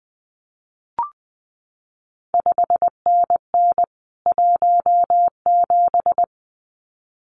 Morse code, heard as a keyed tone, about 19 wpm, 700 Hz. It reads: 5NN 17
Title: CQ WW DX Contest Exchange Encore - 20wpm